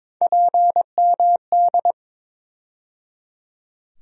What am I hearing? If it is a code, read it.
Morse code, heard as a keyed tone, about 22 wpm, 700 Hz. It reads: PMD
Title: USA Call Sign Postfixes - 3 Letters 22wpm